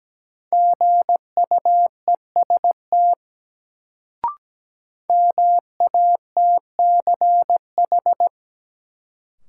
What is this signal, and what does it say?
Morse code, heard as a keyed tone, about 17 wpm, 700 Hz. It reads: GUEST MATCH
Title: Sentences from Top 1600 Words - New Word Drill - 17wpm